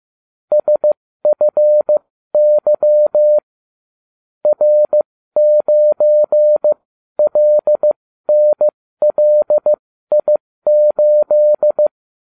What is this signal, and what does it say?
Morse code, heard as a keyed tone, about 15 wpm, spaced as for 13 wpm, 600 Hz. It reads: SFY R9LNLI8